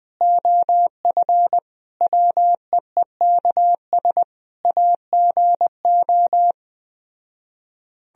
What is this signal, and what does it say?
Morse code, heard as a keyed tone, about 21 wpm, 700 Hz. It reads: OF WEEKS AGO